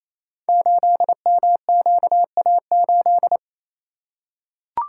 Morse code, heard as a keyed tone, about 27 wpm, 700 Hz. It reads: 8MQA8
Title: Code Groups - 5 Letter:Numbers - 28wpm